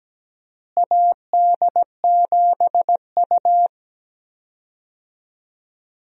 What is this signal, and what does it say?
Morse code, heard as a keyed tone, about 17 wpm, 700 Hz. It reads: AD7U
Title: USA Call Signs - 4 Characters - Repeated 3x - 17wpm